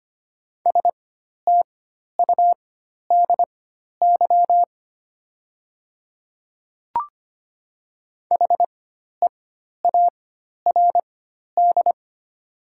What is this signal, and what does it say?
Morse code, heard as a keyed tone, about 25 wpm, spaced as for 12 wpm, 700 Hz. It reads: STUDY HEARD
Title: 4x Character Spacing - Top 500 Words - 25wpm